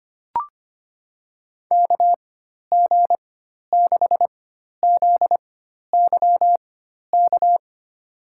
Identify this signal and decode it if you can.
Morse code, heard as a keyed tone, about 25 wpm, spaced as for 12 wpm, 700 Hz. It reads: KG6ZYK
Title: USA Call Signs - 4x Character Spacing - 25wpm